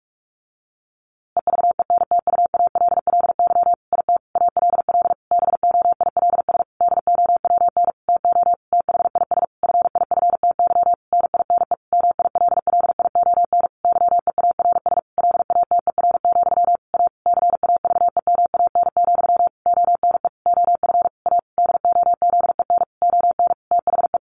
Morse code, heard as a keyed tone, about 46 wpm, 700 Hz. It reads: EVENTUALLY IT ALL BOILS DOWN TO THIS FIFTY NINE MILLION YEARS LATER, A CAVEMAN, ONE OF A DOZEN ON THE
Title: A Sound Of Thunder 45wpm